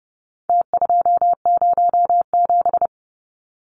Morse code, heard as a keyed tone, about 30 wpm, 700 Hz. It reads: T207
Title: Numbers - Contest Style - 4-Digit 30wpm